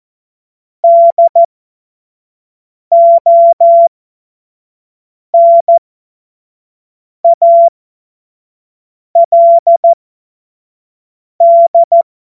Morse code, heard as a keyed tone, about 14 wpm, spaced as for 5 wpm, 680 Hz. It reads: DONALD
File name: Morsekurs / Witze / Witz 09. November 2021, 25 F70